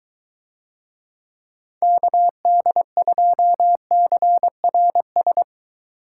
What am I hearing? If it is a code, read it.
Morse code, heard as a keyed tone, about 23 wpm, 700 Hz. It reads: KD2CRS